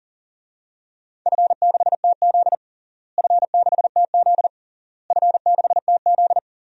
Morse code, heard as a keyed tone, about 40 wpm, 700 Hz. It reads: F6TZ F6TZ F6TZ
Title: DXCC-100 Call Signs - 4 Characters - Repeated 3x - 40wpm